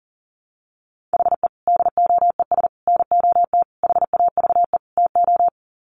Morse code, heard as a keyed tone, about 37 wpm, 700 Hz. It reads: HE DOES NOT HAVE TO
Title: Sets of 5 Words Encore 40wpm